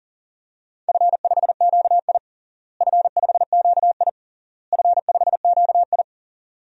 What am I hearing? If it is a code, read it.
Morse code, heard as a keyed tone, about 40 wpm, 700 Hz. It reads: F5QI F5QI F5QI
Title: DXCC-100 Call Signs - 4 Characters - Repeated 3x - 40wpm